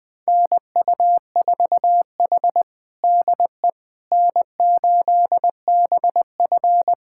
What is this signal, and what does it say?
Morse code, heard as a keyed tone, about 20 wpm, 700 Hz. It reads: NU4H DE N8BF
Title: USA Callsign DE Callsign - 4 Characters - 20wpm